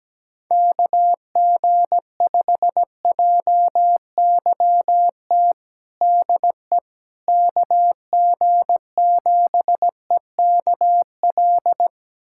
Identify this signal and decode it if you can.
Morse code, heard as a keyed tone, about 17 wpm, 700 Hz. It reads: KG5JYT DE KG7EKL
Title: USA Callsign DE Callsign 17wpm